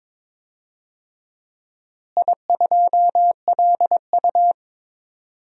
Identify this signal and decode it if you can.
Morse code, heard as a keyed tone, about 22 wpm, 700 Hz. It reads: I2LU
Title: DXCC-100 Call Signs - 4 Characters - Repeated 2x - 22wpm